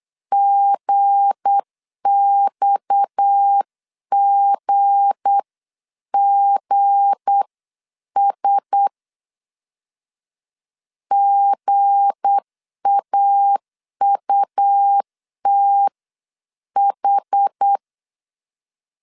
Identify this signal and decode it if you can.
Morse code, heard as a keyed tone, about 8 wpm, spaced as for 7 wpm, 800 Hz. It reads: GXGGS GAUTH
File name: LEZIONI / 7 GF9Z / 7 G 8wpm 3118